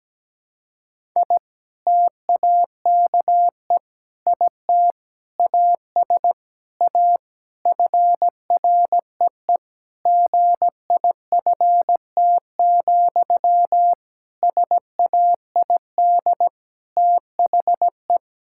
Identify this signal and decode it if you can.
Morse code, heard as a keyed tone, about 17 wpm, 700 Hz. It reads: I TAKE IT AS A FREE GIFT, SAID THE
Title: Sentences from Top 1000 Words - Review Part 6 - 17wpm